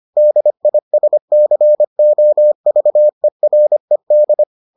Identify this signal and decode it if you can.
Morse code, heard as a keyed tone, about 25 wpm, 600 Hz. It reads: DISCOVERED